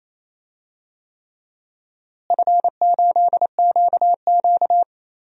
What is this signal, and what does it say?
Morse code, heard as a keyed tone, about 28 wpm, 700 Hz. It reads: F8QQ
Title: DXCC-100 Call Signs - 4 Characters - Repeated 3x - 28wpm